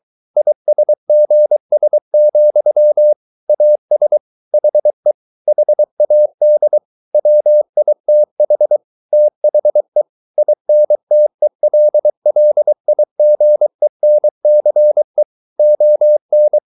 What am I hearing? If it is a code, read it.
Morse code, heard as a keyed tone, about 23 wpm, 600 Hz. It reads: ISGS, AS HE HAD WITH THE INTELLIGENCE ON